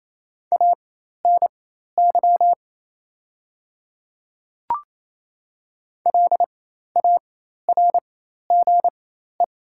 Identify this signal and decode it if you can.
Morse code, heard as a keyed tone, about 28 wpm, spaced as for 13 wpm, 690 Hz. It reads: ANY LARGE
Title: 4x Character Spacing - Top 200 Words - 28wpm